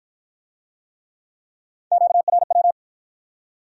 Morse code, heard as a keyed tone, about 51 wpm, 700 Hz. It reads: QRW